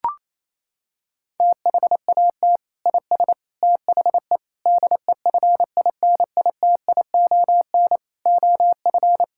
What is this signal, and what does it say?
Morse code, heard as a keyed tone, about 28 wpm, 700 Hz. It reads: THAT IS THE DEFINITION OF
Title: Sentences from Top 1000 Words - Review Part 6 - 28wpm